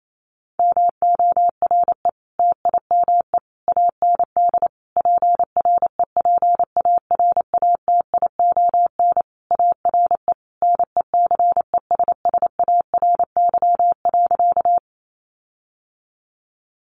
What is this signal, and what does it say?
Morse code, heard as a keyed tone, about 28 wpm, 700 Hz. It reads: MORE TIME AND PREPARATION ARE NECESSARY.
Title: Sentences from Top 1400 Words 28wpm